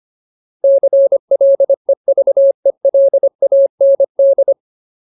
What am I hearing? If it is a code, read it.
Morse code, heard as a keyed tone, about 25 wpm, 550 Hz. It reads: CLEVELAND
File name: Ultimate Word List / 25WPM Ultimate